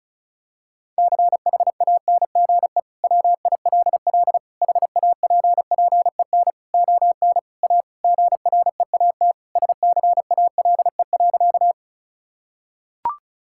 Morse code, heard as a keyed tone, about 36 wpm, 700 Hz. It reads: CHANGE WILL HAPPEN ON A GREAT SCALE.
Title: Sentences from Top 1000 Words - Review Part 5 - 35wpm